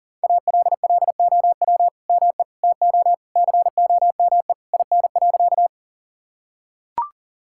Morse code, heard as a keyed tone, about 37 wpm, 700 Hz. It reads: ALLOW ME TO COME IN.
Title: Sentences from Top 1000 Words - Review Part 5 - 40wpm